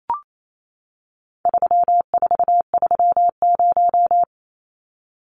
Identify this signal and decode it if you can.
Morse code, heard as a keyed tone, about 28 wpm, 700 Hz. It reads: 3430
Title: Numbers - 4-Digit 28wpm